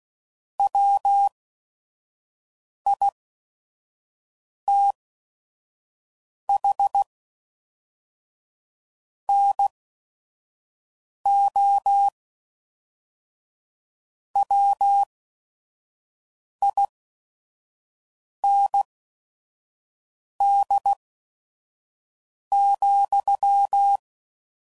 Morse code, heard as a keyed tone, about 16 wpm, spaced as for 5 wpm, 800 Hz. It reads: WITH NO WIND,
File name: CW Code Groups @5WPM / Weather Sets / Weather 3 17